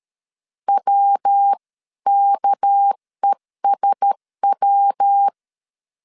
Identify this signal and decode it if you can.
Morse code, heard as a keyed tone, about 13 wpm, 800 Hz. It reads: WKESW
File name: CW / DET / 1 6 1500C S C 12wpm 3533